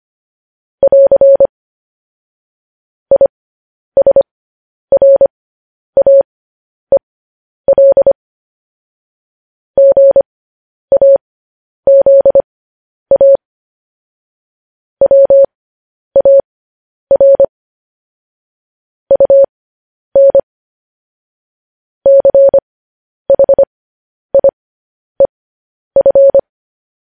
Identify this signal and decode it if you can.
Morse code, heard as a keyed tone, about 25 wpm, spaced as for 10 wpm, 550 Hz. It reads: <AR> ISRAEL GAZA WAR UN CHIEF
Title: News Headlines at 10